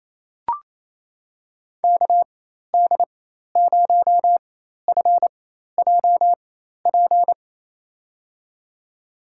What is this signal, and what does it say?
Morse code, heard as a keyed tone, about 28 wpm, spaced as for 13 wpm, 700 Hz. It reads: KD0FJP
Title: USA Call Signs - 6 Characters - 4x Character Spacing - 28wpm